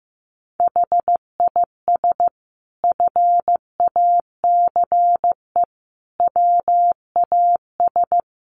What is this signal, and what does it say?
Morse code, heard as a keyed tone, about 14 wpm, 700 Hz. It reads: HIS FACE WAS
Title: Sentences from Top 1200 Words 15wpm